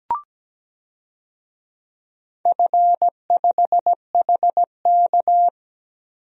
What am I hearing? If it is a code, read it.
Morse code, heard as a keyed tone, about 17 wpm, 700 Hz. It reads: F5HK
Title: DXCC-100 Call Signs - 4 Characters - Repeated 2x - 17wpm